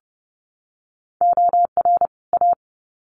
Morse code, heard as a keyed tone, about 30 wpm, 700 Hz. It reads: OR A
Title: Sets of 2 Words 30wpm